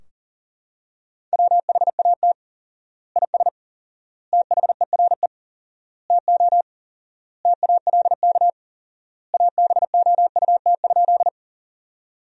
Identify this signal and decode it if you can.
Morse code, heard as a keyed tone, about 37 wpm, 700 Hz. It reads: WHAT IS THERE TO TALK ABOUT?
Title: 4x Word-Spacing - Sentences from Top 400 Words - 40wpm